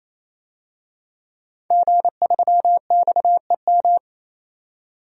Morse code, heard as a keyed tone, about 28 wpm, 700 Hz. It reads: G3XEM
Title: DXCC-100 Call Signs - 5 Characters - 28wpm